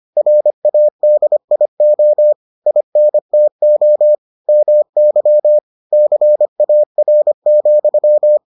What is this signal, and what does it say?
Morse code, heard as a keyed tone, about 25 wpm, 600 Hz. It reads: RADIO INTO MY CAR,